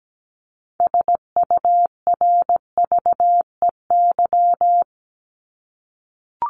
Morse code, heard as a keyed tone, about 17 wpm, 700 Hz. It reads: SURVEY
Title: Top 1501-1750 Words 17wpm